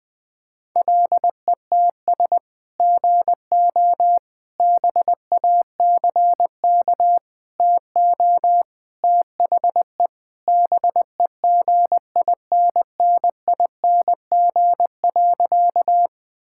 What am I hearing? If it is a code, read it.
Morse code, heard as a keyed tone, about 21 wpm, 700 Hz. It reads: LETS GO BACK TO THE BEGINNING.
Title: Sentences with 3-Word Phrases - Spoken Beginning - 20wpm